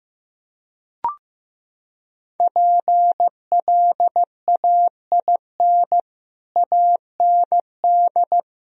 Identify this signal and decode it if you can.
Morse code, heard as a keyed tone, about 16 wpm, 700 Hz. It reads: PLAIN AND
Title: Sentences from Top 1000 Words - Review Part 2 - 15wpm